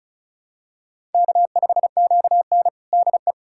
Morse code, heard as a keyed tone, about 36 wpm, 700 Hz. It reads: K5QN DE
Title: USA Callsign DE Callsign - 4 Characters - 35wpm